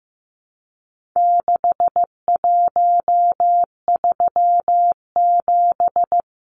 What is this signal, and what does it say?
Morse code, heard as a keyed tone, about 15 wpm, 700 Hz. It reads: 6137
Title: Numbers - 4-Digit 15wpm